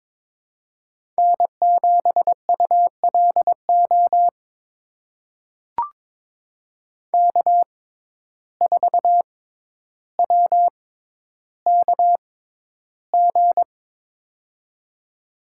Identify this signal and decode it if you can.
Morse code, heard as a keyed tone, about 22 wpm, spaced as for 11 wpm, 700 Hz. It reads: N7ULO K4WKG
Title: USA Call Signs - 5 Characters - 6x Character Spacing - 22wpm